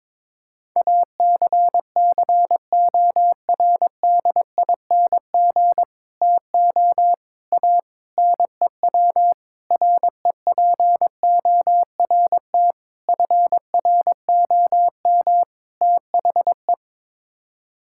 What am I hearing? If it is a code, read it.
Morse code, heard as a keyed tone, about 22 wpm, 700 Hz. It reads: ACCORDING TO A NEW REPORT FROM THE
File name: practice / 22wpm / Sets of 7 Words - 22wpm